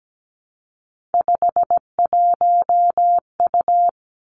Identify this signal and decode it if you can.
Morse code, heard as a keyed tone, about 17 wpm, 700 Hz. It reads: 51U
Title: Three Challenging Letter-Number Combinations 17wpm